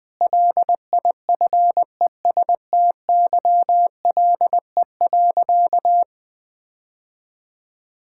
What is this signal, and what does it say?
Morse code, heard as a keyed tone, about 20 wpm, 700 Hz. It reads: LIFESTYLE.
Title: Difficult Sentences with Hints - 20wpm